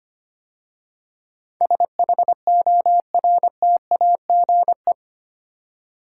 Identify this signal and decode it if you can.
Morse code, heard as a keyed tone, about 25 wpm, 700 Hz. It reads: SHORTAGE
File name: practice / 25wpm / 8 Letter Words - First 3 Letters Given - 25wpm